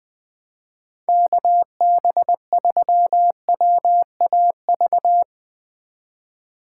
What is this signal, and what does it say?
Morse code, heard as a keyed tone, about 20 wpm, 700 Hz. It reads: KB3WAV
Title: 2023 POTA Activator Call Signs - 20wpm